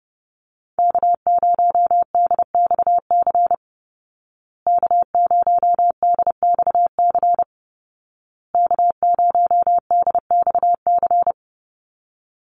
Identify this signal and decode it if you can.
Morse code, heard as a keyed tone, about 28 wpm, 700 Hz. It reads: K0DXC K0DXC K0DXC
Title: Call Signs Repeated 2x - 30wpm